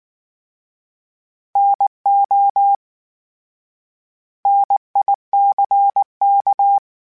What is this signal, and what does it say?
Morse code, heard as a keyed tone, about 19 wpm, 800 Hz. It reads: NO NICK